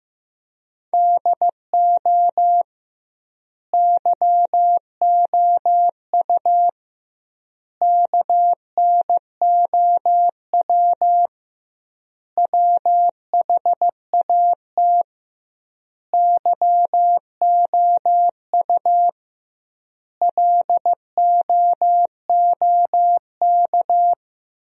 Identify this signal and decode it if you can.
Morse code, heard as a keyed tone, about 16 wpm, 700 Hz. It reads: DO YOU KNOW WHAT YOU LOOK
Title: 2x Word-Spacing - Sentences from Top 100 Words - 15wpm